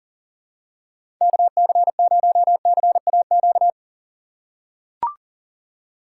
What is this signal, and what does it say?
Morse code, heard as a keyed tone, about 40 wpm, 700 Hz. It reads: KC0CAQ
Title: USA Call Signs - 6 Characters - 2x Character Spacing - 40wpm